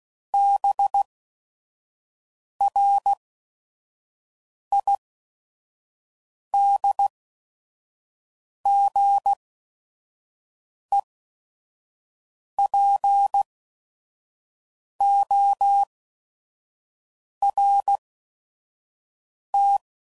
Morse code, heard as a keyed tone, about 16 wpm, spaced as for 5 wpm, 800 Hz. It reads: BRIDGEPORT
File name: CW Code Groups @5WPM / fromSM's / Cities